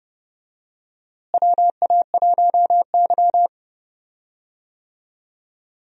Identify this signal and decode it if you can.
Morse code, heard as a keyed tone, about 30 wpm, 700 Hz. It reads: WA1Y
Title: USA Call Signs - 4 Characters - Repeated 2x - 30wpm